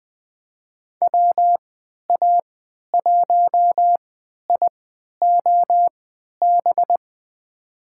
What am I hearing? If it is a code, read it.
Morse code, heard as a keyed tone, about 20 wpm, spaced as for 11 wpm, 700 Hz. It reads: WA1IOB
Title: USA Call Signs - 6 Characters - 3x Character Spacing - 20wpm